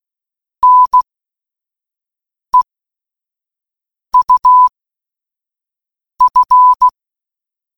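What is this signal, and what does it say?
Morse code, heard as a keyed tone, about 16 wpm, spaced as for 5 wpm, 1000 Hz. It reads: NEUF
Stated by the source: Morse code puzzle in an escape game: three coded words giving a padlock's three digits.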